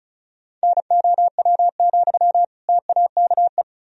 Morse code, heard as a keyed tone, about 36 wpm, 700 Hz. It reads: NOW, TAKE
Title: Sentences from Top 1000 Words - Review Part 2 - 35wpm